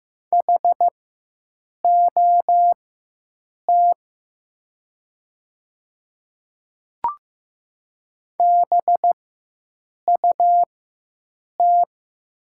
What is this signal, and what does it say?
Morse code, heard as a keyed tone, about 15 wpm, spaced as for 7 wpm, 700 Hz. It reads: HOT BUT
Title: 4x Character Spacing - Top 100 Words - 15wpm